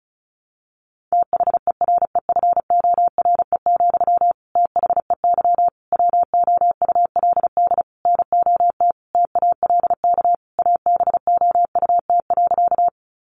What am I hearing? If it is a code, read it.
Morse code, heard as a keyed tone, about 34 wpm, 700 Hz. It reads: THEREFORE, THEY WOULD NOT TALK ABOUT.
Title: Sentences from Top 800 Words 35wpm